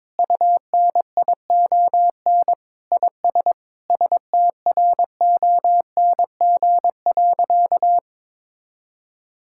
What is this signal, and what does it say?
Morse code, heard as a keyed tone, about 22 wpm, 700 Hz. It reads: UNION IS STRONG.
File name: practice / 22wpm / Sentences with 3-Word Phrases - 22wpm